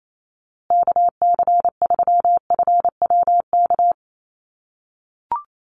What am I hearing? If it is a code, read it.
Morse code, heard as a keyed tone, about 28 wpm, 700 Hz. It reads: KC3FWK
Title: USA Call Signs Repeated 2x 28wpm